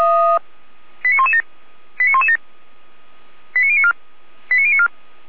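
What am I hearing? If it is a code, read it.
Five-tone selective call (ZVEI), heard as quick runs of five tones, tones of about 70 ms. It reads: 89117 89117 89094 89094
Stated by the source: ZVEI five-tone selective call 89115, received by SDR.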